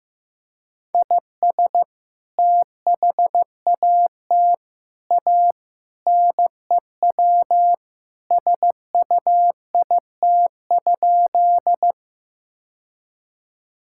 Morse code, heard as a keyed tone, about 16 wpm, 700 Hz. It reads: IS THAT A NEW SUIT?
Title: Sentences from Top 1600 Words 15wpm